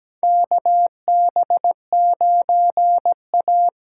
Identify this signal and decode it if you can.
Morse code, heard as a keyed tone, about 17 wpm, 700 Hz. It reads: KB9A
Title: USA Call Signs - 8x Character Spacing - 17wpm